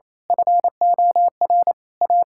Morse code, heard as a keyed tone, about 28 wpm, 700 Hz. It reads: FOR A
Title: Pangram Challenge v2 - 28wpm